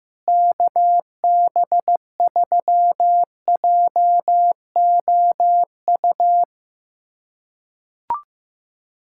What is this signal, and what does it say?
Morse code, heard as a keyed tone, about 15 wpm, 700 Hz. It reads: KB3JOU
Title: USA Callsign DE Callsign - 6 Characters - 15wpm